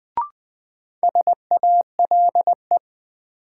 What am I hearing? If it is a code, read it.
Morse code, heard as a keyed tone, about 20 wpm, 700 Hz. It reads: SALE
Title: Sentences from Top 1400 Words - New Word Drill Alt - 20wpm